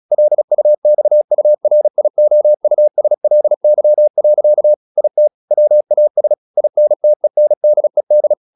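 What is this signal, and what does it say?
Morse code, heard as a keyed tone, about 36 wpm, 600 Hz. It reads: LUXURIOUSLY. IT WAS INTENDED
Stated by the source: Morse practice at 36 wpm, text of Ken Follett's novel Never, from CW med Gnister.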